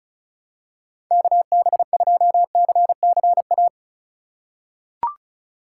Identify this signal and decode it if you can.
Morse code, heard as a keyed tone, about 36 wpm, 700 Hz. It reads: KB2CCA
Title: USA Call Signs - 6 Characters - 8x Character Spacing - 35wpm